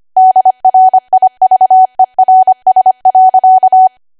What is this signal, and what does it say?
Morse code, heard as a keyed tone, about 25 wpm, 750 Hz. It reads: DRIVERS.
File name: cwa / W1AW-Level3 / 25WPM